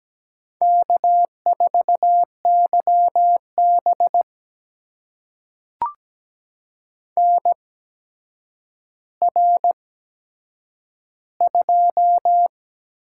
Morse code, heard as a keyed tone, about 17 wpm, spaced as for 8 wpm, 700 Hz. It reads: K4YB NR2
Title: USA Call Signs - 4 Characters - 8x Character Spacing - 17wpm